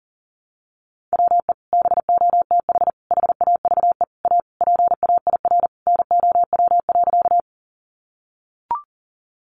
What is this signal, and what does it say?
Morse code, heard as a keyed tone, about 40 wpm, 700 Hz. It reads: WE BOTH HAVE A PAIR NOW.
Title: Sentences from Top 1200 Words 40wpm